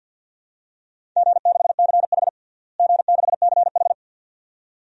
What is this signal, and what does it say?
Morse code, heard as a keyed tone, about 52 wpm, 700 Hz. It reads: G6CH G6CH
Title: DXCC-100 Call Signs - 4 Characters - Repeated 2x - 50wpm